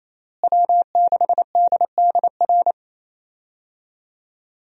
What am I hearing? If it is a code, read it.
Morse code, heard as a keyed tone, about 28 wpm, 700 Hz. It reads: W6DDR
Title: USA Call Signs - 5 Characters - Repeated 3x - 28wpm